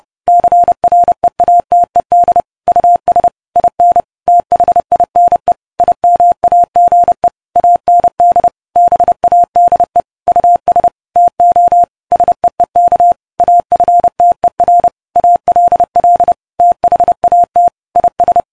Morse code, heard as a keyed tone, about 30 wpm, 700 Hz. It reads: CREATED US IN THINE IMAGE AND BADE US TO SEEK AFTER ALL THAT IS